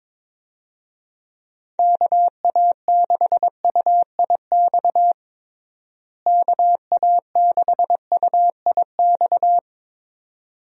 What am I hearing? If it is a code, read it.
Morse code, heard as a keyed tone, about 22 wpm, 700 Hz. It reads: KA6UIX KA6UIX
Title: USA Call Signs - 6 Characters - Repeated 2x - 22wpm